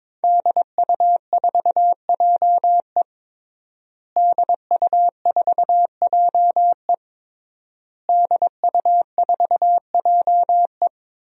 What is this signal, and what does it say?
Morse code, heard as a keyed tone, about 22 wpm, 700 Hz. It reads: DU4JE DU4JE DU4JE
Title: DXCC-100 Call Signs - 5 Characters - Repeated 3x - 22wpm